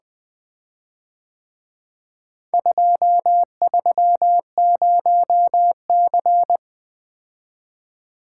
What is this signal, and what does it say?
Morse code, heard as a keyed tone, about 23 wpm, 700 Hz. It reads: 230C
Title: Speed-Racing - Four Letter-Number Combinations - 20wpm